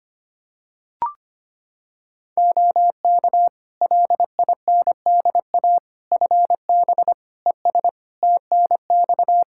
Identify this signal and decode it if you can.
Morse code, heard as a keyed tone, about 25 wpm, 700 Hz. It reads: OK LINDA FB ES TNX
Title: Practice QSO - Exchange 2 - Info - 25wpm